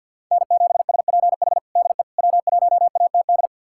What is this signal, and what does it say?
Morse code, heard as a keyed tone, about 49 wpm, 700 Hz. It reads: N7SPH DE W1ATD
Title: USA Callsign DE Callsign - 5 Characters - 50wpm